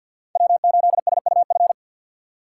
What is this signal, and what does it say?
Morse code, heard as a keyed tone, about 47 wpm, 700 Hz. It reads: W8SUF
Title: USA Call Signs - 3x Character Spacing - 50wpm